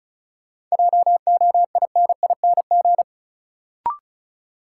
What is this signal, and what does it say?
Morse code, heard as a keyed tone, about 36 wpm, 700 Hz. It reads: JOINING
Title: Sets of 5 Words - New Word Drill - 35wpm